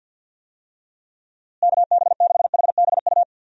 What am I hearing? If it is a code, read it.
Morse code, heard as a keyed tone, about 52 wpm, 700 Hz. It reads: KB6HBU
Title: USA Call Signs - 6 Characters - Repeated 2x - 50wpm